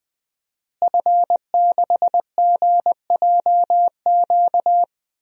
Morse code, heard as a keyed tone, about 20 wpm, 700 Hz. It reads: F6GJQ
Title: DXCC-100 Call Signs - 5 Characters - Repeated 2x - 20wpm